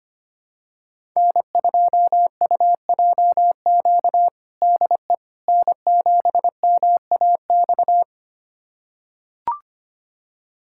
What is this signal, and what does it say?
Morse code, heard as a keyed tone, about 25 wpm, 700 Hz. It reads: N2UJQ DE N7MAX
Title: USA Callsign DE Callsign - 5 Characters - 25wpm